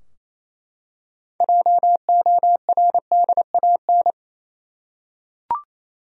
Morse code, heard as a keyed tone, about 28 wpm, 700 Hz. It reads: JORDAN
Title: USA Common Names - 28wpm